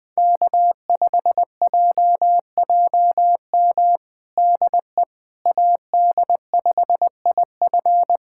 Morse code, heard as a keyed tone, about 20 wpm, 700 Hz. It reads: K5JJM DE AD5IF
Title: USA Callsign DE Callsign - 5 Characters - 20wpm